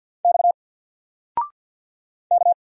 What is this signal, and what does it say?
Morse code, heard as a keyed tone, about 50 wpm, 700 Hz. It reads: X K
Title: Single Letter-Number 50wpm